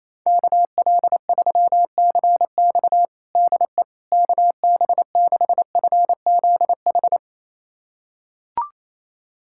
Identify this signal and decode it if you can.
Morse code, heard as a keyed tone, about 28 wpm, 700 Hz. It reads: KL3CX DE KB6FZH
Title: USA Callsign DE Callsign 28wpm